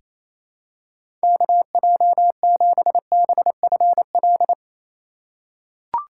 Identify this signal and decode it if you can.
Morse code, heard as a keyed tone, about 28 wpm, 700 Hz. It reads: KJ7BFL
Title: USA Call Signs - 6 Characters - 4x Character Spacing - 28wpm